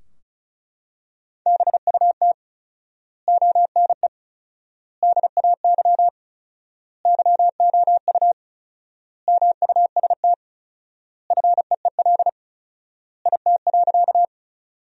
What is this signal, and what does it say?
Morse code, heard as a keyed tone, about 36 wpm, 700 Hz. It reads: BUT ONE DAY YOU MUST FEEL IT.
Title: 4x Word-Spacing - Sentences from Top 400 Words - 35wpm